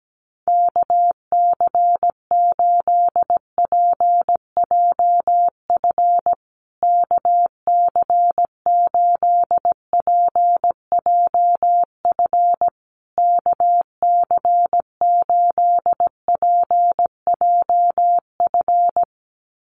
Morse code, heard as a keyed tone, about 17 wpm, 700 Hz. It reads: KC8PJF KC8PJF KC8PJF
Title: USA Call Signs Repeated 3x 17wpm